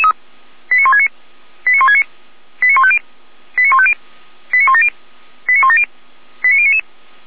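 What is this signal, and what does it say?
Five-tone selective call (ZVEI), heard as quick runs of five tones, tones of about 70 ms. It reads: 89169 89169 89150 89150 89170 89170 89099
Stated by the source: Five-tone ZVEI selective call received by SDR, from the Feuerwehr Orlamünde (f85360) audio archive.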